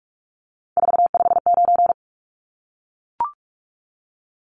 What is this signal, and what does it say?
Morse code, heard as a keyed tone, about 44 wpm, 700 Hz. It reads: 459
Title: Numbers - 3-Digit 45wpm